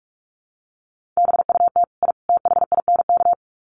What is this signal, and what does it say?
Morse code, heard as a keyed tone, about 44 wpm, 700 Hz. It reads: BUT I THINK
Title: Sets of 3 Words 45wpm